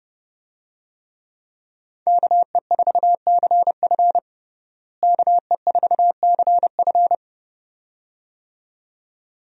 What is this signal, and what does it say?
Morse code, heard as a keyed tone, about 32 wpm, 700 Hz. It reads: KE4CF KE4CF
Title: USA Call Signs - 5 Characters - Repeated 2x - 30wpm